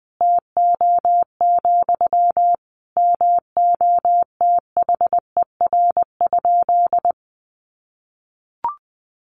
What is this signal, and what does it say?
Morse code, heard as a keyed tone, about 20 wpm, 700 Hz. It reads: TO, MOTHER?
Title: Sentences from Top 1000 Words 20wpm